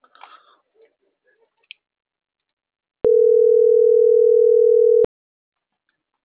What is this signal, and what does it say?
Ringback tone (North America), 440 + 480 Hz.